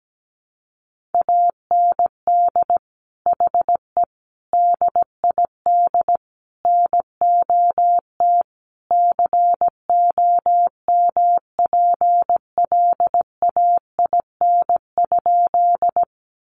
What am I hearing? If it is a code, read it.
Morse code, heard as a keyed tone, about 17 wpm, 700 Hz. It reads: AND HE DID NOT COMPLAIN?
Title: Sentences from Top 1300 Words 17wpm